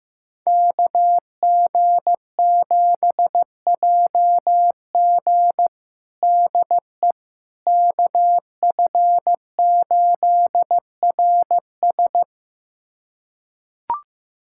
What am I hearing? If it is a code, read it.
Morse code, heard as a keyed tone, about 15 wpm, 700 Hz. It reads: KG7JG DE KF8RS
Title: USA Callsign DE Callsign - 5 Characters - 15wpm